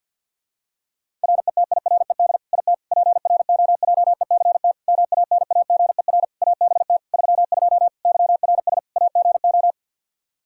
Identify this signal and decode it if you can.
Morse code, heard as a keyed tone, about 53 wpm, 700 Hz. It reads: RETIRED IT PROJECT MANAGER ABT 32 YRS AGO
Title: Practice QSO - Exchange 3 - More Info - 50wpm